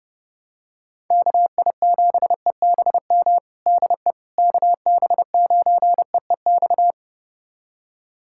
Morse code, heard as a keyed tone, about 30 wpm, 700 Hz. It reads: KI7EBM DE KB9EEX
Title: USA Callsign DE Callsign - 6 Characters - 30wpm